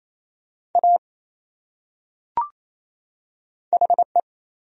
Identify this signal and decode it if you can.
Morse code, heard as a keyed tone, about 28 wpm, 700 Hz. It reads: A HE
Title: Top 100 Words 28wpm